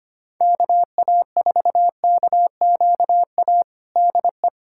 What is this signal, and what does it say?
Morse code, heard as a keyed tone, about 25 wpm, 700 Hz. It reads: KA4KQA DE
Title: USA Callsign DE Callsign 25wpm